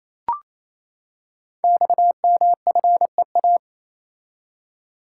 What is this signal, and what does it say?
Morse code, heard as a keyed tone, about 27 wpm, 700 Hz. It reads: XMFEA
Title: Code Groups - 5 Letter:Numbers - 28wpm